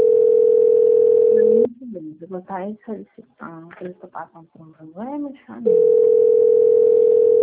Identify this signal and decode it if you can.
Ringback tone (North America), 440 + 480 Hz, with speech.
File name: out- - - - - .4997